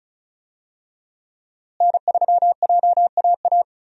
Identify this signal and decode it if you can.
Morse code, heard as a keyed tone, about 36 wpm, 700 Hz. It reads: N3JAA
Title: USA Call Signs - 5 Characters - Repeated 2x - 35wpm